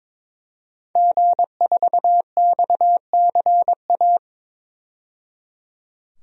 Morse code, heard as a keyed tone, about 22 wpm, 700 Hz. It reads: G4XCA
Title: DXCC-100 Call Signs - 5 Characters - 22wpm